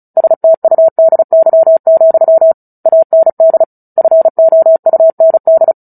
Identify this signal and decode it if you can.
Morse code, heard as a keyed tone, about 34 wpm, 650 Hz. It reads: STUDY, AND FOUND
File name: Book Two / 09-Chapter 9 - 35 WPM